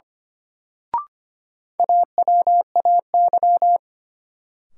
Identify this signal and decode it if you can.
Morse code, heard as a keyed tone, about 25 wpm, 700 Hz. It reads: AWAY
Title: Sets of 2 Words - New Word Drill - 25wpm